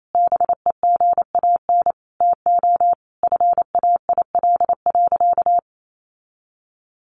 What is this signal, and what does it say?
Morse code, heard as a keyed tone, about 28 wpm, 700 Hz. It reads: BEGAN TO FAIL.
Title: Sentences from Top 1100 Words 28wpm